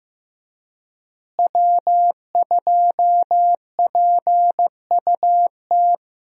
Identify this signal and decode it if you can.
Morse code, heard as a keyed tone, about 15 wpm, 700 Hz. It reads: W2PUT
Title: USA Call Signs - 5 Characters - Repeated 2x - 15wpm